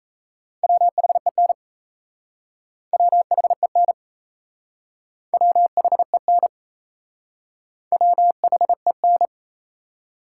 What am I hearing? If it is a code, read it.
Morse code, heard as a keyed tone, about 36 wpm, 700 Hz. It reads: WHEN WHEN WHEN WHEN